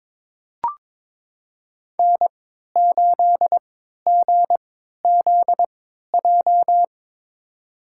Morse code, heard as a keyed tone, about 23 wpm, spaced as for 13 wpm, 700 Hz. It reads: N8GZJ
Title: USA Call Signs - 5 Characters - 3x Character Spacing - 22wpm